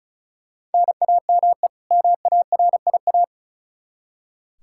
Morse code, heard as a keyed tone, about 36 wpm, 700 Hz. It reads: NAME MARIA
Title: Practice QSO - Exchange 1 - Beginning - 35wpm